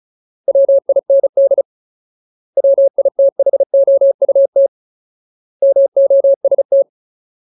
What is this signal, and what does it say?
Morse code, heard as a keyed tone, about 36 wpm, 550 Hz. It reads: WIND WITHOUT MOST
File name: Ultimate Word List / 35WPM Ultimate